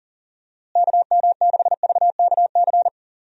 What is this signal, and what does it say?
Morse code, heard as a keyed tone, about 40 wpm, 700 Hz. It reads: KM6VKC
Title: USA Call Signs - 6 Characters - Repeated 2x - 40wpm